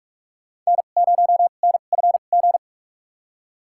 Morse code, heard as a keyed tone, about 44 wpm, spaced as for 32 wpm, 700 Hz. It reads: N0NFG
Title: USA Call Signs - 5 Characters - 2x Character Spacing - 45wpm